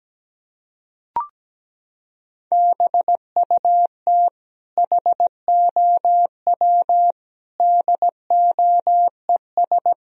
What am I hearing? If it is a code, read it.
Morse code, heard as a keyed tone, about 17 wpm, 700 Hz. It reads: BUT HOW DOES
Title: Sentences from Top 1000 Words - Review Part 3 - 17wpm